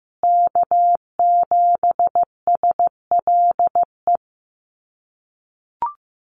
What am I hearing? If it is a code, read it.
Morse code, heard as a keyed tone, about 15 wpm, 700 Hz. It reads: K7SLE
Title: USA Call Signs Repeated 3x 15wpm